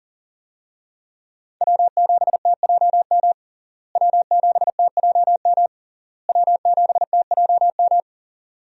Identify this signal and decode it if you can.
Morse code, heard as a keyed tone, about 40 wpm, 700 Hz. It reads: W7TJM W7TJM W7TJM
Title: USA Call Signs - 5 Characters - Repeated 3x - 40wpm